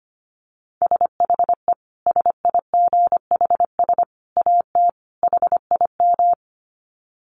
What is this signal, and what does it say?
Morse code, heard as a keyed tone, about 25 wpm, 700 Hz. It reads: SHE SIGHS AT HIM
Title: Dit Challenge 25wpm